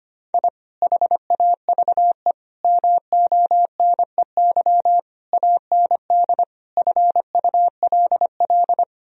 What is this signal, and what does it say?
Morse code, heard as a keyed tone, about 25 wpm, 700 Hz. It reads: I HAVE MONEY AND FULL
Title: Sentences from Top 1000 Words - Review Part 3 - 25wpm